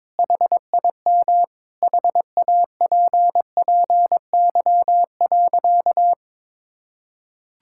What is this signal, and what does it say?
Morse code, heard as a keyed tone, about 22 wpm, 700 Hz. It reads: HIM HAPPY.
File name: practice / 22wpm / Sentences with 2-Word Phrases - 22wpm